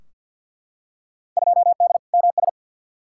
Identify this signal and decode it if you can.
Morse code, heard as a keyed tone, about 52 wpm, 700 Hz. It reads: 2D MS
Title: Field Day Exchanges Encore - 50wpm